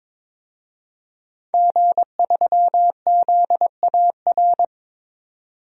Morse code, heard as a keyed tone, about 22 wpm, 700 Hz. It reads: G3ZAR
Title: DXCC-100 Call Signs - 5 Characters - 22wpm